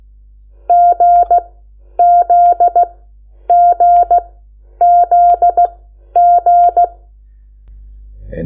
Morse code, heard as a keyed tone, about 16 wpm, spaced as for 10 wpm, 700 Hz. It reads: GZGZG
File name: Lecciones / CLASE Nº8